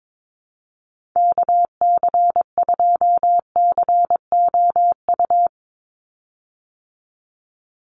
Morse code, heard as a keyed tone, about 22 wpm, 700 Hz. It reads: KC2COU
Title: USA Call Signs 22wpm